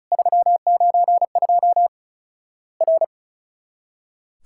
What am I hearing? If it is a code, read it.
Morse code, heard as a keyed tone, about 36 wpm, 700 Hz. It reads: 392 R
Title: Worked All Europe DX Contest - QTC Exchange - 35wpm